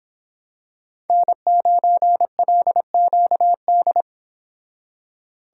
Morse code, heard as a keyed tone, about 26 wpm, 700 Hz. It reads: N9LQD
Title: Speed Racing - USA 1x3 Call Signs 22wpm